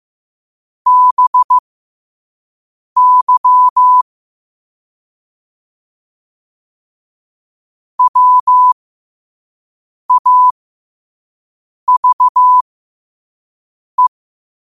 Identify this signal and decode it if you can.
Morse code, heard as a keyed tone, about 15 wpm, spaced as for 5 wpm, 1000 Hz. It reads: BY WAVE
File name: qotdfiles / 5wpm